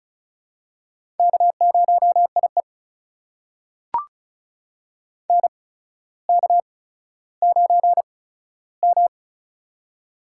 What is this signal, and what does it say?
Morse code, heard as a keyed tone, about 36 wpm, spaced as for 15 wpm, 700 Hz. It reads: K0IE NK9M
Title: USA Call Signs - 4 Characters - 8x Character Spacing - 35wpm